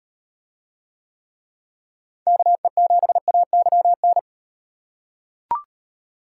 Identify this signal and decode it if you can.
Morse code, heard as a keyed tone, about 38 wpm, 700 Hz. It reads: KE7AYN